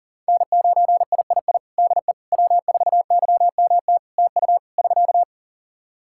Morse code, heard as a keyed tone, about 37 wpm, 700 Hz. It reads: N9III DE W4YMT TU <SK>
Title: Practice QSO - Exchange 4 - Ending - 40wpm